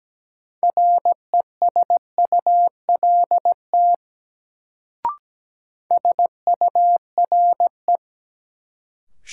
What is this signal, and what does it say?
Morse code, heard as a keyed tone, about 17 wpm, 700 Hz. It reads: RESULT SURE
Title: Sets of 3 Words - New Word Drill - 17wpm